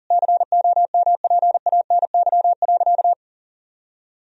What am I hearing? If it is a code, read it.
Morse code, heard as a keyed tone, about 40 wpm, 700 Hz. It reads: COMPANY.
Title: Sentences from Top 1000 Words - Review Part 3 - 40wpm